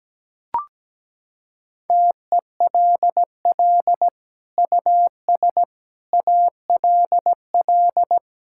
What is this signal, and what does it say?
Morse code, heard as a keyed tone, about 17 wpm, 700 Hz. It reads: TELL US ALL
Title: Sentences from Top 1000 Words - Review Part 5 - 17wpm